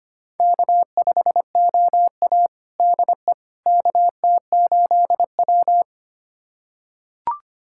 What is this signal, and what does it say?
Morse code, heard as a keyed tone, about 25 wpm, 700 Hz. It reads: K5OA DE KT8W
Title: USA Callsign DE Callsign - 4 Characters - 25wpm